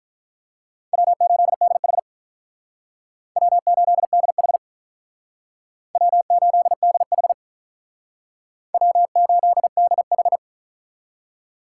Morse code, heard as a keyed tone, about 45 wpm, 700 Hz. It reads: W8DH W8DH W8DH W8DH